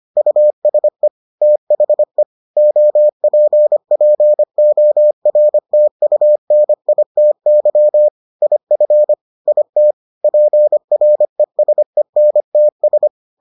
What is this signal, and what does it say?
Morse code, heard as a keyed tone, about 25 wpm, 600 Hz. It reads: USE THE OPPORTUNITY IF IT PRESENTS